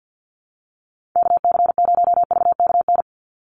Morse code, heard as a keyed tone, about 50 wpm, 700 Hz. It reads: KC0VKN
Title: Call Signs - 50wpm